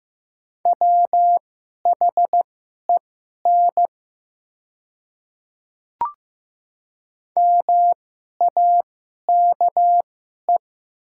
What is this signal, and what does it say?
Morse code, heard as a keyed tone, about 15 wpm, spaced as for 11 wpm, 700 Hz. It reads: WHEN MAKE
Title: 2x Character Spacing - Top 100 Words - 15wpm